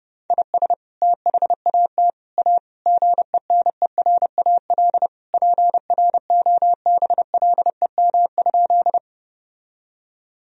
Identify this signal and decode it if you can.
Morse code, heard as a keyed tone, about 32 wpm, 700 Hz. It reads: IS THAT A GENERAL PROBLEM?
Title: Sentences from Top 1000 Words - Review Part 3 - 30wpm